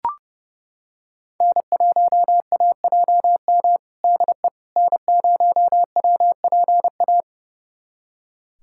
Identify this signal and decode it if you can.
Morse code, heard as a keyed tone, about 30 wpm, 700 Hz. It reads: N1AJM DE N0WPA
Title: USA Callsign DE Callsign 30wpm